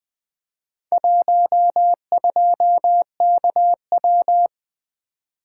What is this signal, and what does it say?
Morse code, heard as a keyed tone, about 24 wpm, 700 Hz. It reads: 12KW
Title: Speed-Racing - Four Letter-Number Combinations - 20wpm